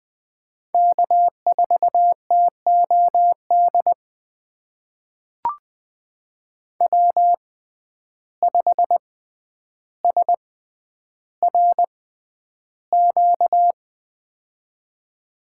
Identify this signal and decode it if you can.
Morse code, heard as a keyed tone, about 20 wpm, spaced as for 10 wpm, 700 Hz. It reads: K4TOD W5SRQ
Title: USA Call Signs - 5 Characters - 6x Character Spacing - 20wpm